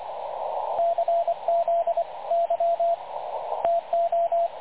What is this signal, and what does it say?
Morse code, heard as a keyed tone, about 25 wpm, 680 Hz, with radio noise. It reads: CZY TO
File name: dlazaawansowanych / nagraniazpasma / websdr recording 3567.0kHz